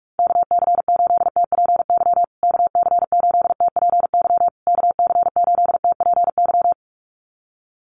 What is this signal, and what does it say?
Morse code, heard as a keyed tone, about 44 wpm, 700 Hz. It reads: KC8TPY KC8TPY KC8TPY
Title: USA Call Signs Repeated 3x 45wpm